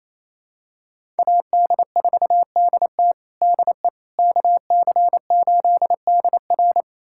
Morse code, heard as a keyed tone, about 27 wpm, 690 Hz. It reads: AD4DT DE KC8DR
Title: USA Callsign DE Callsign - 5 Characters - 28wpm